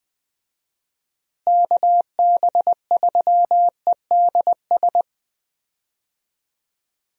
Morse code, heard as a keyed tone, about 20 wpm, 700 Hz. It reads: KB3EDS
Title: USA Call Signs - 6 Characters - Repeated 3x - 20wpm